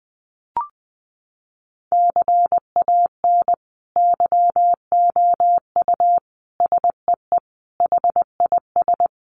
Morse code, heard as a keyed tone, about 20 wpm, 700 Hz. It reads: CAN YOU SEE HIS
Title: Sentences from Top 1400 Words 20wpm